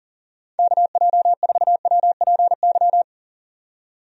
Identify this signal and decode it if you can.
Morse code, heard as a keyed tone, about 40 wpm, 700 Hz. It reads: KJ4WPY
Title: USA Call Signs - 6 Characters - 40wpm